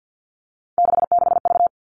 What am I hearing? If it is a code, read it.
Morse code, heard as a keyed tone, about 50 wpm, 700 Hz. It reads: 66V
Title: All Cut Numbers - 3-Digit 50wpm